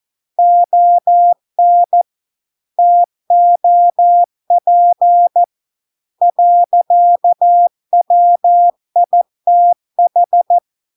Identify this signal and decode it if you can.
Morse code, heard as a keyed tone, about 14 wpm, 700 Hz. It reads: ON TOP . WITH